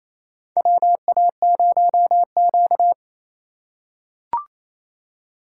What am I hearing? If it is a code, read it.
Morse code, heard as a keyed tone, about 28 wpm, 700 Hz. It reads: WA0Q
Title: USA Call Signs - 4 Characters - 3x Character Spacing - 28wpm